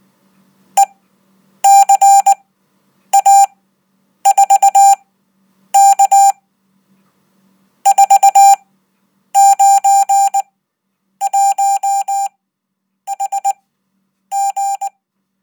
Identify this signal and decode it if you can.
Morse code, heard as a keyed tone, about 19 wpm, spaced as for 8 wpm, 790 Hz. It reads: ECA4K 491HG
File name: Code